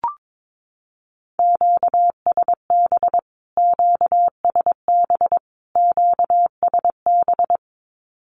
Morse code, heard as a keyed tone, about 22 wpm, 700 Hz. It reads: QSB QSB QSB
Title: Q-Codes Repeated 3x 22wpm